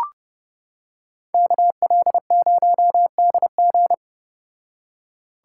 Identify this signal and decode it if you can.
Morse code, heard as a keyed tone, about 31 wpm, 700 Hz. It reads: KL0DG